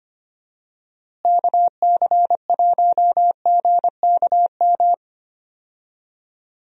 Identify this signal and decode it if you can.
Morse code, heard as a keyed tone, about 25 wpm, 700 Hz. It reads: KC1GKM
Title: USA Call Signs - 6 Characters - Repeated 2x - 25wpm